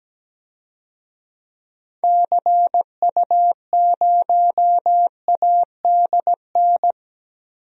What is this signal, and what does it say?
Morse code, heard as a keyed tone, about 17 wpm, 700 Hz. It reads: CU0ADN
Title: DXCC-100 Call Signs - 6 Characters - Repeated 3x - 17wpm